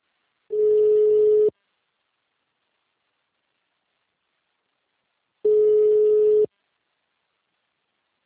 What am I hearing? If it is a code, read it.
Ringback tone (Europe, 425 Hz).